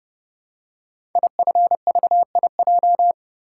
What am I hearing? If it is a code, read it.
Morse code, heard as a keyed tone, about 30 wpm, 700 Hz. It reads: IFVIJ
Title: Code Groups - 5 Letters - 30wpm